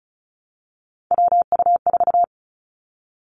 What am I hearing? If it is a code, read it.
Morse code, heard as a keyed tone, about 36 wpm, 700 Hz. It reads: WU4
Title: Three Challenging Letter-Number Combinations 35wpm